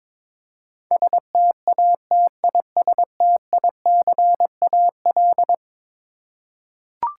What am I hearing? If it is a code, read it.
Morse code, heard as a keyed tone, about 21 wpm, 700 Hz. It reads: STATISTICAL
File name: practice / 22wpm / Eleven Letter Words - 22wpm